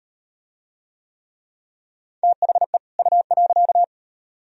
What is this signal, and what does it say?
Morse code, heard as a keyed tone, about 36 wpm, 700 Hz. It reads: THE U.